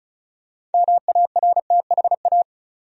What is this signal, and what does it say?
Morse code, heard as a keyed tone, about 35 wpm, 700 Hz. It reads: MARTHA
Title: USA Common Names - 35wpm